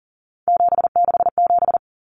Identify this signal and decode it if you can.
Morse code, heard as a keyed tone, about 40 wpm, 700 Hz. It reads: 767
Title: Common Cut Numbers - 3-Digit 40wpm